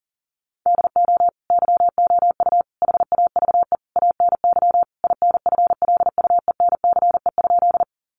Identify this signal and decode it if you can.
Morse code, heard as a keyed tone, about 40 wpm, 700 Hz. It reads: DO YOU HAVE ANY INFLUENCE?
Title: Sentences from Top 1100 Words 40wpm